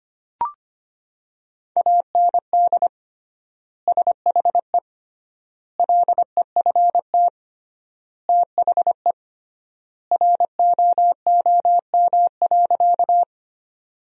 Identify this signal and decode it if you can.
Morse code, heard as a keyed tone, about 25 wpm, 700 Hz. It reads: AND SHE LEFT THE ROOM.
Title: 3x Word-Spacing - Sentences from Top 300 Words - 25wpm